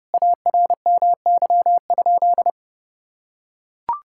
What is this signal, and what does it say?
Morse code, heard as a keyed tone, about 30 wpm, 700 Hz. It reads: ARMY?
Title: Sentences from Top 1000 Words - Review Part 6 - 30wpm